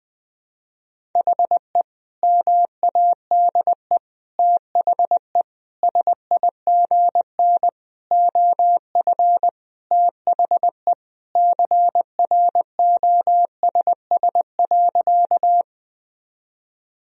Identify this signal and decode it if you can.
Morse code, heard as a keyed tone, about 20 wpm, 700 Hz. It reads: HE MADE THE SIGN OF THE CROSS.
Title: Sentences from Top 1000 Words - Review Part 1 - 20wpm